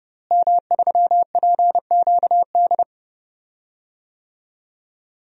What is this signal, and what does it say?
Morse code, heard as a keyed tone, about 30 wpm, 700 Hz. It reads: M3PQD
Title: DXCC-100 Call Signs - 5 Characters - Repeated 3x - 30wpm